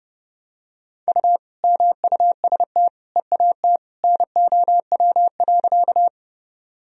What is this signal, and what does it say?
Morse code, heard as a keyed tone, about 30 wpm, 700 Hz. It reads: U MUST EAT NOW.
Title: Sentences from Top 300 Words and Abbreviations 30wpm